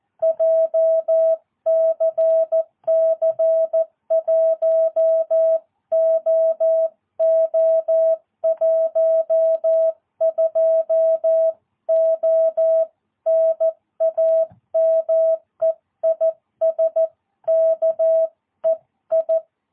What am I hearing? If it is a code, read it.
Morse code, heard as a keyed tone, about 14 wpm, 650 Hz. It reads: JCC 1OO12O NAME IS KEI
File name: jh8cbh / 2022cw